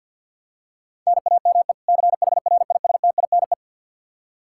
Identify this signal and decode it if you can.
Morse code, heard as a keyed tone, about 49 wpm, 700 Hz. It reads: NAME CHRISTINE
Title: Practice QSO - Exchange 1 - Beginning - 50wpm